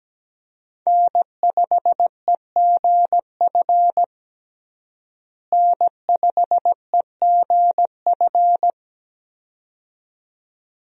Morse code, heard as a keyed tone, about 17 wpm, 700 Hz. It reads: N5EGF N5EGF
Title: USA Call Signs - 5 Characters - Repeated 2x - 17wpm